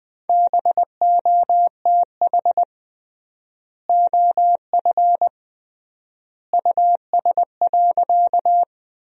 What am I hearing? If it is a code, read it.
Morse code, heard as a keyed tone, about 21 wpm, 700 Hz. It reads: BOTH OF US.
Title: 3x Word-Spacing - Sentences from Top 300 Words - 20wpm